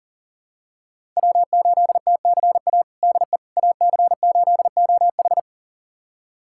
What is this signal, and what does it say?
Morse code, heard as a keyed tone, about 40 wpm, 700 Hz. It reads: W8TCA DE AC8OH
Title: USA Callsign DE Callsign - 5 Characters - 40wpm